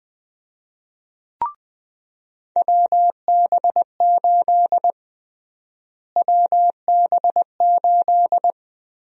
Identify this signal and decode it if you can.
Morse code, heard as a keyed tone, about 20 wpm, 700 Hz. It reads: WB8 WB8
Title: USA Call Sign Prefixes - Repeated 2x - 20wpm